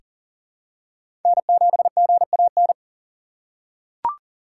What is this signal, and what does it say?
Morse code, heard as a keyed tone, about 40 wpm, 700 Hz. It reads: N7GAN
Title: USA Call Signs - 5 Characters - 6x Character Spacing - 40wpm